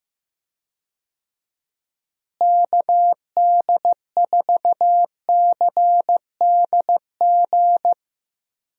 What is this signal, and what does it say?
Morse code, heard as a keyed tone, about 15 wpm, 700 Hz. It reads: KD4CDG
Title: USA Call Signs - 6 Characters - Repeated 2x - 15wpm